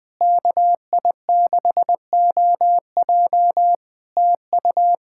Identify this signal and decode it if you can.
Morse code, heard as a keyed tone, about 20 wpm, 700 Hz. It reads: KI6OJ TU
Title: Practice QSO - Exchange 4 - Ending - 20wpm